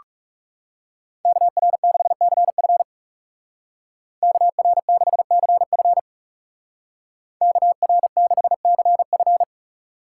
Morse code, heard as a keyed tone, about 36 wpm, 700 Hz. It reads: KR6CF KR6CF KR6CF
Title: Speed Racing - USA Call Signs 30wpm